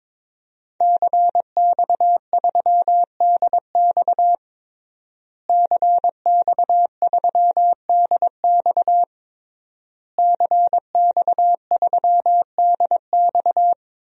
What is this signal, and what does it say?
Morse code, heard as a keyed tone, about 22 wpm, 700 Hz. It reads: CX3DX CX3DX CX3DX
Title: DXCC-100 Call Signs - 5 Characters - Repeated 3x - 22wpm